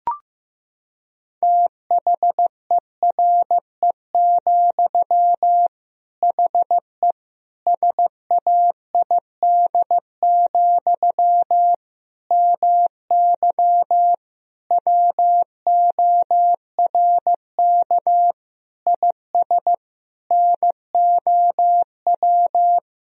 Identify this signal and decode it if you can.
Morse code, heard as a keyed tone, about 15 wpm, 700 Hz. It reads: THERE, HE SAID, MY WORK IS NOW
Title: Sentences from Top 1000 Words - Review Part 2 - 15wpm